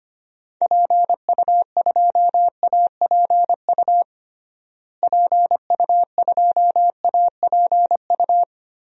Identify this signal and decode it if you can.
Morse code, heard as a keyed tone, about 25 wpm, 700 Hz. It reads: PU2APU PU2APU
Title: DXCC-100 Call Signs - 6 Characters - Repeated 2x - 25wpm